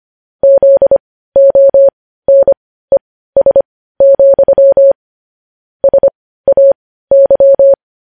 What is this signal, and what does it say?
Morse code, heard as a keyed tone, about 25 wpm, spaced as for 15 wpm, 550 Hz. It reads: ZONES, SAY
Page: news headlines in morse code at 15 wpm